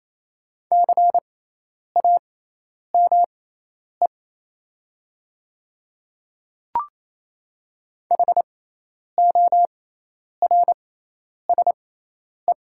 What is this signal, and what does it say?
Morse code, heard as a keyed tone, about 28 wpm, spaced as for 10 wpm, 700 Hz. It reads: CAME HORSE
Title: 6x Character Spacing - Top 500 Words - 28wpm